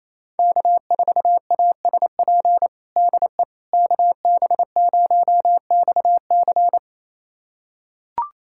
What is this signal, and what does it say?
Morse code, heard as a keyed tone, about 28 wpm, 700 Hz. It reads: K4ASP DE KB0XC
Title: USA Callsign DE Callsign - 5 Characters - 28wpm